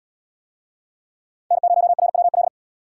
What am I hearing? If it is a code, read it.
Morse code, heard as a keyed tone, about 65 wpm, 700 Hz. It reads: N0RWL